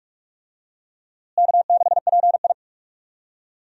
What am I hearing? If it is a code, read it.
Morse code, heard as a keyed tone, about 46 wpm, 700 Hz. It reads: K6PI
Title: USA Call Signs - 4 Characters - Repeated 2x - 45wpm